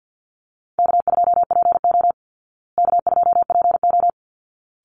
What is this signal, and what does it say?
Morse code, heard as a keyed tone, about 49 wpm, 700 Hz. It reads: K2PO K2PO
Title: Easy Call Signs Repeated 2x - 50wpm